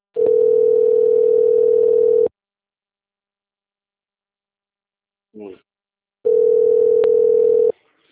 Ringback tone (North America), 440 + 480 Hz.